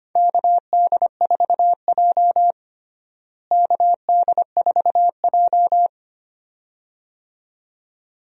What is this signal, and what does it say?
Morse code, heard as a keyed tone, about 25 wpm, 700 Hz. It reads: KD4J KD4J
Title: USA Call Signs - 4 Characters - Repeated 2x - 25wpm